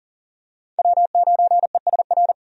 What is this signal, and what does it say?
Morse code, heard as a keyed tone, about 40 wpm, 700 Hz. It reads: W9ESR
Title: USA Call Signs - 5 Characters - 40wpm